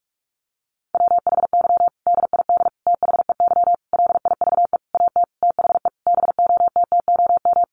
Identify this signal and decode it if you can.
Morse code, heard as a keyed tone, about 44 wpm, 700 Hz. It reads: WHY DID THEY LIVE AT THE BOTTOM
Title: Sentences from Top 900 Words 45wpm